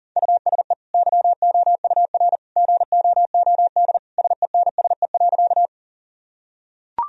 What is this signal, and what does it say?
Morse code, heard as a keyed tone, about 40 wpm, 700 Hz. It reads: USE YOUR GOOD SENSE.
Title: Sentences from Top 1000 Words - Review Part 4 - 40wpm